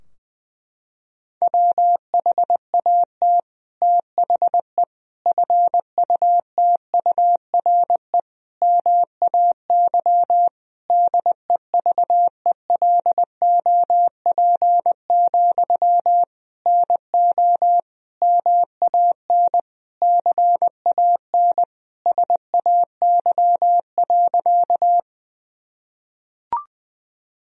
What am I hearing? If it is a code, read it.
Morse code, heard as a keyed tone, about 20 wpm, 700 Hz. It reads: WHAT THE FUTURE MAY DEVELOP, NO MAN CAN SAY.
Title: Sentences from Top 1000 Words - Review Part 2 - 20wpm